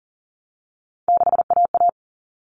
Morse code, heard as a keyed tone, about 40 wpm, 700 Hz. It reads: 6AA
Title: All Cut Numbers - 3-Digit 40wpm